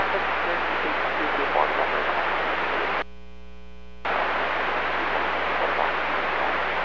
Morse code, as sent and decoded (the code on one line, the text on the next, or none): none